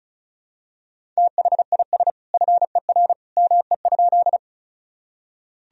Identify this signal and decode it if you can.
Morse code, heard as a keyed tone, about 34 wpm, 700 Hz. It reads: THIS FER ME?
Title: Sentences from Top 200 Words and Abbreviations 35wpm